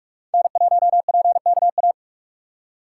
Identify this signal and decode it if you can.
Morse code, heard as a keyed tone, about 41 wpm, 700 Hz. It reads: N1PKA